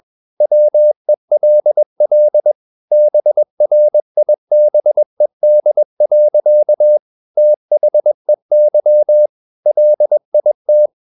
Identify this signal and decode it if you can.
Morse code, heard as a keyed tone, about 21 wpm, 600 Hz. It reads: WELL BRIBED. THEY LIT
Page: CW med Gnister